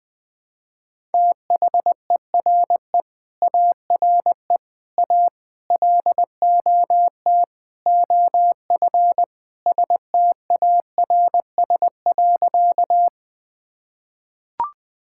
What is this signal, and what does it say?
Morse code, heard as a keyed tone, about 20 wpm, 700 Hz. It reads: THERE ARE A LOT OF STARS.
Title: Sentences with 4-Word Phrases - 20wpm